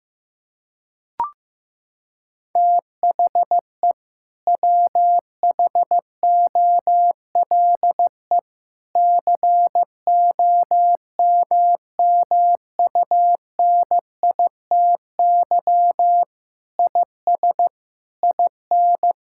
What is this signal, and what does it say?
Morse code, heard as a keyed tone, about 15 wpm, 700 Hz. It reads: THE WHOLE COMMUNITY IS IN
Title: Sentences from Top 1600 Words 15wpm